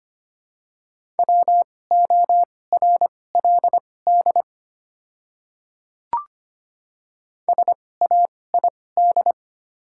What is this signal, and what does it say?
Morse code, heard as a keyed tone, about 25 wpm, spaced as for 18 wpm, 700 Hz. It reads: WORLD SAID
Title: 2x Character Spacing - Top 300 Words - 25wpm